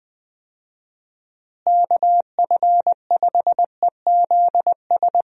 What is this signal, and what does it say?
Morse code, heard as a keyed tone, about 20 wpm, 700 Hz. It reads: KF5EZS
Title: USA Call Signs - 6 Characters - Repeated 3x - 20wpm